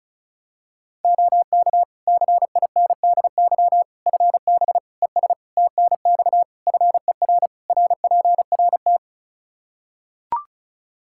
Morse code, heard as a keyed tone, about 34 wpm, 700 Hz. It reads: OK CINDY FB ES TNX FER RPRT
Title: K1USN SST QSO Practice 35wpm